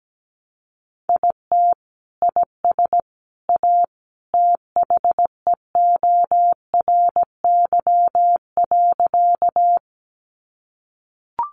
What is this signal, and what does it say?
Morse code, heard as a keyed tone, about 17 wpm, 700 Hz. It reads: IT IS A THEORY.
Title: Sentences from Top 600 Words 17wpm